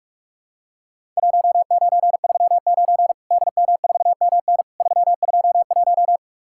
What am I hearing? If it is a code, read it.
Morse code, heard as a keyed tone, about 46 wpm, 700 Hz. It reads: 1939 DM4MN 321
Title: Worked All Europe DX Contest - QTC Exchange - 45wpm